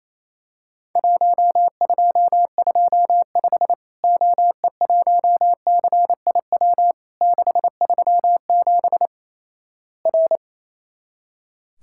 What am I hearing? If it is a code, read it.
Morse code, heard as a keyed tone, about 27 wpm, 700 Hz. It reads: 1225 OE1CIW 637 R
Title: Worked All Europe DX Contest - QTC Exchange - 28wpm